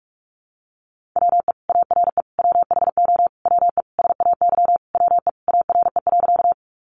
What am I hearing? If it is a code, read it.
Morse code, heard as a keyed tone, about 44 wpm, 700 Hz. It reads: WE ARE WHO WE SAY WE ARE.
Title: Sentences from Top 200 Words 45wpm